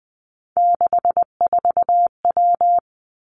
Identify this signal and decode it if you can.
Morse code, heard as a keyed tone, about 20 wpm, 700 Hz. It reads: 64W
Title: Three Challenging Letter-Number Combinations 20wpm